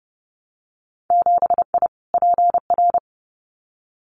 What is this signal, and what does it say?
Morse code, heard as a keyed tone, about 30 wpm, 700 Hz. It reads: 7I PR
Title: Winter Field Day Exchanges Encore 30wpm